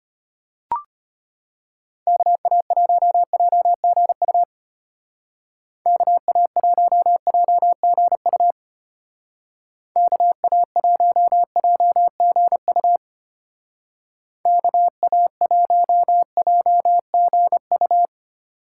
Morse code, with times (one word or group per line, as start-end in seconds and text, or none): 2.1-4.4 KA1JGU
5.9-8.5 KA1JGU
10.0-13.0 KA1JGU
14.5-18.1 KA1JGU